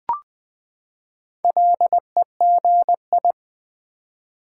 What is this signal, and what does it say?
Morse code, heard as a keyed tone, about 20 wpm, 700 Hz. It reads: LEGI
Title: Build A Word - 4 Letters At A Time - 11-Letter Words - 20wpm